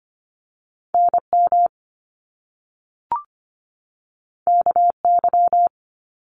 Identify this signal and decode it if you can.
Morse code, heard as a keyed tone, about 25 wpm, 700 Hz. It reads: NM KY
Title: US States 25wpm